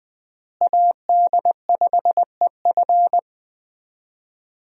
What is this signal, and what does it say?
Morse code, heard as a keyed tone, about 20 wpm, 700 Hz. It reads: AD5EF
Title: USA Call Signs - 5 Characters - Repeated 2x - 20wpm